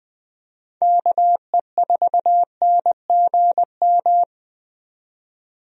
Morse code, heard as a keyed tone, about 20 wpm, 700 Hz. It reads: KE4NGM
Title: USA Call Signs - 6 Characters - 20wpm